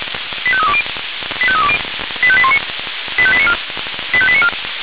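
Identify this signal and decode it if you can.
Five-tone selective call (ZVEI), heard as quick runs of five tones, tones of about 70 ms. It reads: 85320 85320 85819 85894 85894
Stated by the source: ZVEI five-tone selective call received on an SDR.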